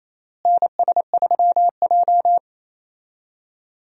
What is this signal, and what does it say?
Morse code, heard as a keyed tone, about 28 wpm, 700 Hz. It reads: NS3J
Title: USA Call Signs - 4 Characters - Repeated 3x - 28wpm